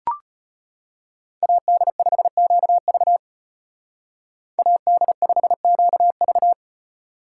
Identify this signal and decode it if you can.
Morse code, heard as a keyed tone, about 30 wpm, 700 Hz. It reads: AD5QV AD5QV
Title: Speed Racing - USA 2x2 Call Signs 25wpm